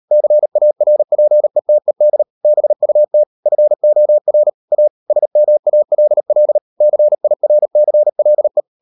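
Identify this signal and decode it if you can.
Morse code, heard as a keyed tone, about 38 wpm, 600 Hz. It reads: CARPETED BUT FOR A SMALL CIRCLE
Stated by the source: Morse code practice at 38 WPM, sending Ken Follett's novel Never in English, from CW med Gnister.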